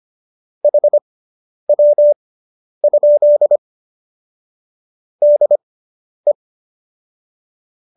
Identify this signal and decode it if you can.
Morse code, heard as a keyed tone, about 25 wpm, spaced as for 10 wpm, 590 Hz. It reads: HW? DE